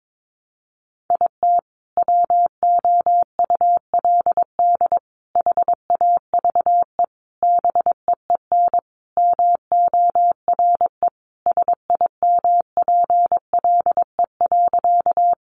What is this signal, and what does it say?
Morse code, heard as a keyed tone, about 22 wpm, 700 Hz. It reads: IT WOULD HAVE BEEN MORE SIMPLE.
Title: Sentences from Top 400 Words 22wpm